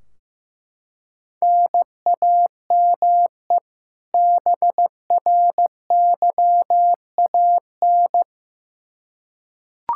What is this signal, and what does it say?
Morse code, heard as a keyed tone, about 16 wpm, 700 Hz. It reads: NAME BRYAN
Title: K1USN SST QSO Practice 15wpm